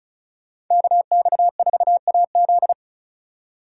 Morse code, heard as a keyed tone, about 46 wpm, 700 Hz. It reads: KX4AZ